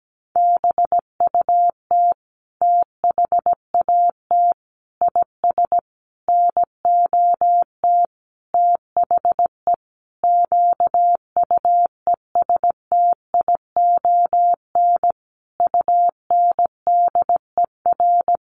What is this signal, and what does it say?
Morse code, heard as a keyed tone, about 17 wpm, 700 Hz. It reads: BUT THAT IS NOT THE QUESTION UNDER
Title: Sentences from Top 1500 Words 17wpm